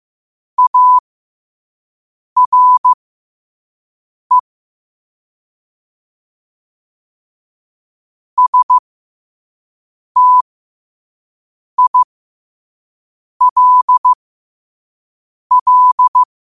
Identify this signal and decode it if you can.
Morse code, heard as a keyed tone, about 15 wpm, spaced as for 5 wpm, 1000 Hz. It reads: ARE STILL
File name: 5wpm